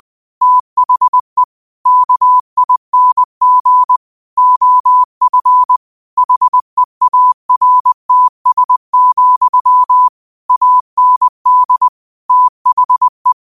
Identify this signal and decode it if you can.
Morse code, heard as a keyed tone, about 20 wpm, 1000 Hz. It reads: THE KING OF HEARTS, AND THE
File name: qotdfiles / 20wpm